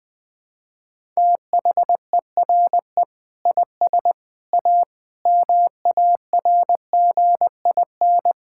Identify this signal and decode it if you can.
Morse code, heard as a keyed tone, about 21 wpm, 700 Hz. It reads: THERE IS A MARGIN
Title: Sentences with 8-Word Phrases - 20wpm